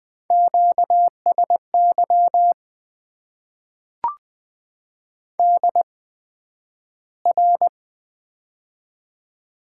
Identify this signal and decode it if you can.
Morse code, heard as a keyed tone, about 20 wpm, spaced as for 11 wpm, 700 Hz. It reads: QSY DR
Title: Top 90 QSO Elements - 8x Character Spacing - 20wpm